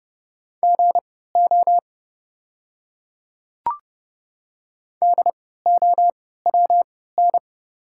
Morse code, heard as a keyed tone, about 30 wpm, spaced as for 17 wpm, 700 Hz. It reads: GO DOWN
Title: 3x Character Spacing - Top 100 Words - 30wpm